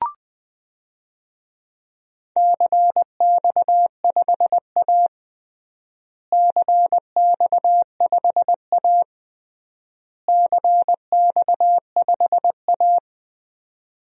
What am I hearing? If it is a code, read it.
Morse code, heard as a keyed tone, about 20 wpm, 700 Hz. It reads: CX5A CX5A CX5A
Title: DXCC-100 Call Signs - 4 Characters - Repeated 3x - 20wpm